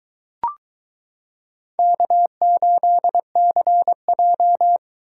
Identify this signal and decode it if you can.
Morse code, heard as a keyed tone, about 22 wpm, 700 Hz. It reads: K8CJ